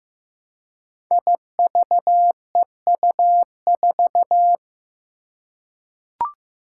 Morse code, heard as a keyed tone, about 15 wpm, 700 Hz. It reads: IVEU4
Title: Five Challenging Letter-Number Combinations 15wpm